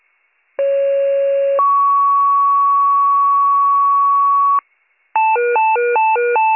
Two-tone page, heard as a one-second tone, then a three-second tone, about 550 then 1120 Hz. It is heard from 0.6 to 4.6 s.